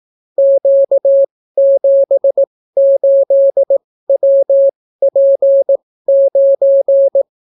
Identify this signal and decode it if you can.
Morse code, heard as a keyed tone, about 18 wpm, spaced as for 14 wpm, 550 Hz. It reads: Q78WP9